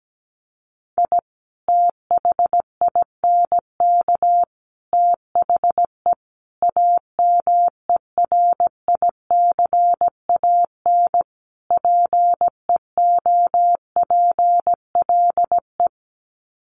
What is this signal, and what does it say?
Morse code, heard as a keyed tone, about 17 wpm, 700 Hz. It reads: I THINK THE AMERICAN PEOPLE
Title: Sets of 5 Words Encore 17wpm